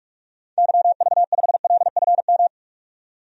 Morse code, heard as a keyed tone, about 46 wpm, 700 Hz. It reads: YV5LFM
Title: DXCC-100 Call Signs - 6 Characters - 45wpm